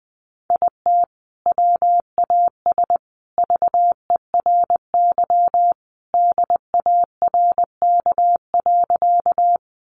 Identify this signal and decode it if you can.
Morse code, heard as a keyed tone, about 20 wpm, 700 Hz. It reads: IT WAS VERY DARK.
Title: Sentences from Top 500 Words 20wpm